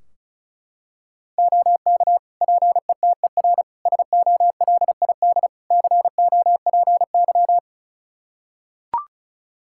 Morse code, heard as a keyed tone, about 34 wpm, 700 Hz. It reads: OK PETER SOLID COPY
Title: K1USN SST QSO Practice 35wpm